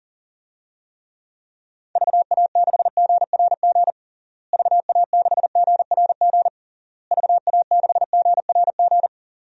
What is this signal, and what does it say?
Morse code, heard as a keyed tone, about 40 wpm, 700 Hz. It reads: VA6GRG VA6GRG VA6GRG
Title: DXCC-100 Call Signs - 6 Characters - Repeated 3x - 40wpm